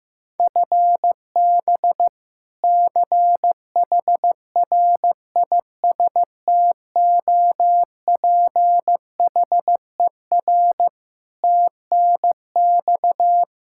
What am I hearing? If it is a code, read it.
Morse code, heard as a keyed tone, about 15 wpm, 700 Hz. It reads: FB CHRISTOPHER TNX
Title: K1USN SST QSO Practice 15wpm